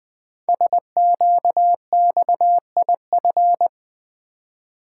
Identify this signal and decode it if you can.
Morse code, heard as a keyed tone, about 21 wpm, 700 Hz. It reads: SQXIF
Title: Code Groups - 5 Letter:Numbers - 20wpm